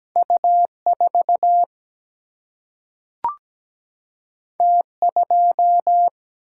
Morse code, heard as a keyed tone, about 17 wpm, 700 Hz. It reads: U 4 T 2
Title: Single-Letter Single-Number 17wpm